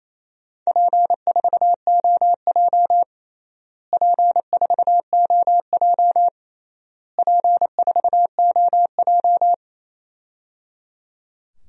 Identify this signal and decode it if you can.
Morse code, heard as a keyed tone, about 28 wpm, 700 Hz. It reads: P4OJ P4OJ P4OJ
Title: DXCC-100 Call Signs - 4 Characters - Repeated 3x - 28wpm